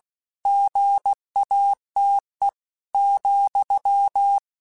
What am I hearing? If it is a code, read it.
Morse code, heard as a keyed tone, about 16 wpm, 800 Hz. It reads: GATE,
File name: Technical Subjects / Technical Subjects-mp3s / AND Gate @15